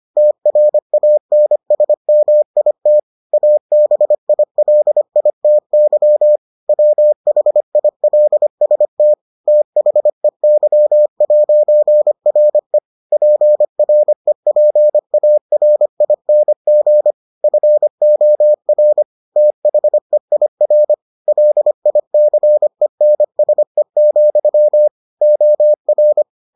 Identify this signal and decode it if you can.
Morse code, heard as a keyed tone, about 25 wpm, 600 Hz. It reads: TRANSMIT ABILITY WHILST THEY'RE PREPARING FOR THEIR LICENSE, OR